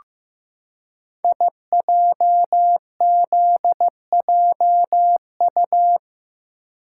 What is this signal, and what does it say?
Morse code, heard as a keyed tone, about 15 wpm, 700 Hz. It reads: IJZJU
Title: Code Groups - 5 Letter:Numbers - 15wpm